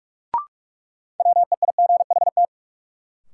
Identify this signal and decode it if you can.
Morse code, heard as a keyed tone, about 44 wpm, 700 Hz. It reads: WEIGHT
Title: Sentences from Top 800 Words - New Word Drill - 45wpm